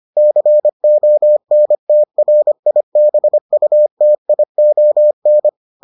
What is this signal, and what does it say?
Morse code, heard as a keyed tone, about 25 wpm, 600 Hz. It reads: CONTRIBUTION